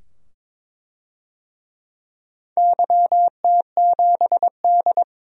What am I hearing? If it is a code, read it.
Morse code, heard as a keyed tone, about 22 wpm, 700 Hz. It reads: YT7D
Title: DXCC-100 Call Signs - 4 Characters - 22wpm